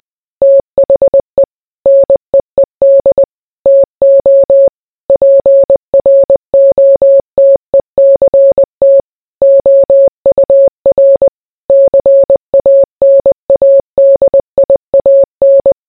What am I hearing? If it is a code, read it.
Morse code, heard as a keyed tone, about 20 wpm, 550 Hz. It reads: THE NEED TO PROTECT OUR CANADIAN